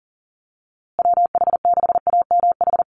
Morse code, heard as a keyed tone, about 40 wpm, 700 Hz. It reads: WH6AMH
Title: USA Call Signs Repeated 3x 40wpm